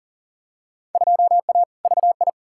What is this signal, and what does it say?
Morse code, heard as a keyed tone, about 37 wpm, 700 Hz. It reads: 2A VI
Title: Field Day Exchanges Encore - 40wpm